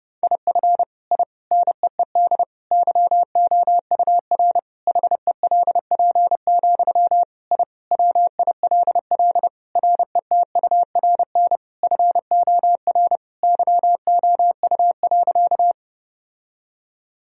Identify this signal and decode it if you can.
Morse code, heard as a keyed tone, about 30 wpm, 700 Hz. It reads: IF I NEED YOUR HELP, I WILL RETURN FOR YOU.
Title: Sentences from Top 1000 Words - Review Part 1 - 30wpm